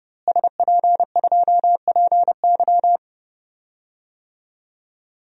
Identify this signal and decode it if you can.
Morse code, heard as a keyed tone, about 31 wpm, 700 Hz. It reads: SP2PY
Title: DXCC-100 Call Signs - 5 Characters - Repeated 2x - 30wpm